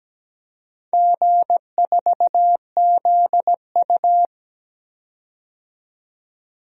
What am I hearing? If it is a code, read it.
Morse code, heard as a keyed tone, about 17 wpm, 700 Hz. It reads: G4ZU
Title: DXCC-100 Call Signs - 4 Characters - 17wpm